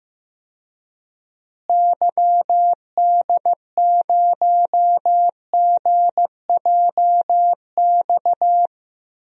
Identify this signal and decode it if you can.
Morse code, heard as a keyed tone, about 15 wpm, 700 Hz. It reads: YD0GJX
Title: DXCC-100 Call Signs - 6 Characters - Repeated 2x - 15wpm